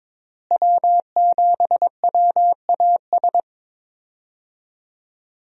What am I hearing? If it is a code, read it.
Morse code, heard as a keyed tone, about 22 wpm, 700 Hz. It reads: W7WAS
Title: USA Call Signs - 5 Characters - Repeated 3x - 22wpm